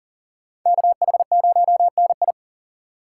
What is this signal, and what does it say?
Morse code, heard as a keyed tone, about 50 wpm, 700 Hz. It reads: KH0NI